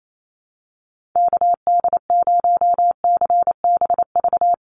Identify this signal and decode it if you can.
Morse code, heard as a keyed tone, about 28 wpm, 700 Hz. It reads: KD0CBV
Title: USA Call Signs 28wpm